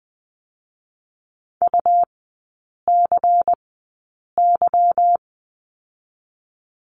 Morse code, heard as a keyed tone, about 20 wpm, spaced as for 8 wpm, 690 Hz. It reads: UCY
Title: 172 Lesson 28e - Number 9 - Three Characters 20wpm